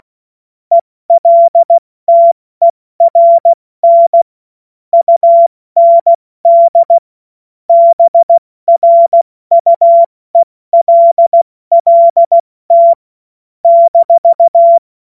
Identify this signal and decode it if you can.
Morse code, heard as a keyed tone, about 16 wpm, spaced as for 14 wpm, 680 Hz. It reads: ELTERN UND BRUELLT -
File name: Morsekurs / Witze / Witz 16. November 2021, 70 F80